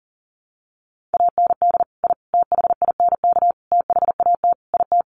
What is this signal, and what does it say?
Morse code, heard as a keyed tone, about 40 wpm, 700 Hz. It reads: AND I THINK THAT IT
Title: Sets of 5 Words Encore2 40wpm